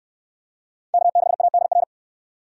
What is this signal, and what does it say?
Morse code, heard as a keyed tone, about 64 wpm, 700 Hz. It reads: K6ADR